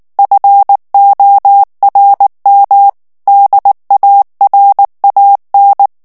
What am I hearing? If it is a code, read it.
Morse code, heard as a keyed tone, about 19 wpm, 800 Hz. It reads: FORM DARAN
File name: morse mp3 / ALLEMAND texte clair 2 vitesse 18 mots minute